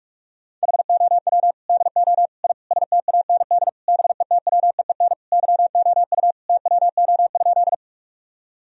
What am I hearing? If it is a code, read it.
Morse code, heard as a keyed tone, about 46 wpm, 700 Hz. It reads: HOW DO I STAND BETWEEN YOU TWO?
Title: Sentences from Top 1000 Words - Review Part 1 - 45wpm